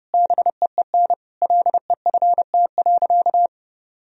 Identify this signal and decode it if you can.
Morse code, heard as a keyed tone, about 30 wpm, 700 Hz. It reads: BEEN LEFT.
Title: Sentences from Top 1000 Words - Review Part 2 - 30wpm